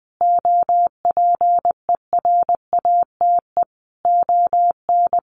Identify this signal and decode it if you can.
Morse code, heard as a keyed tone, about 20 wpm, 700 Hz. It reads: OPERATE ON
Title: Sentences from Top 1500 Words 20wpm